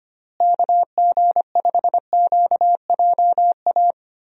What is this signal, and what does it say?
Morse code, heard as a keyed tone, about 25 wpm, 700 Hz. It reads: KG5QJA
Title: USA Call Signs - 6 Characters - Repeated 3x - 25wpm